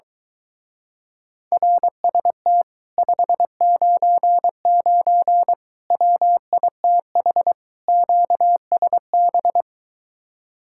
Morse code, heard as a keyed tone, about 22 wpm, 700 Hz. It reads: RST 599 WITH QSB